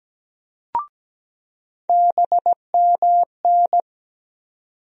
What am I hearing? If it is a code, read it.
Morse code, heard as a keyed tone, about 17 wpm, 700 Hz. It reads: BMN
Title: USA Call Sign Postfixes - 3 Letters 17wpm